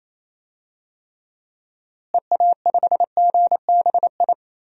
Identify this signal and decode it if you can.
Morse code, heard as a keyed tone, about 28 wpm, 700 Hz. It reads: EA5GBI
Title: DXCC-100 Call Signs - 6 Characters - Repeated 3x - 28wpm